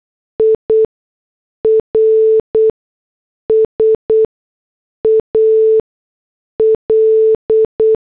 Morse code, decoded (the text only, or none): IRSAL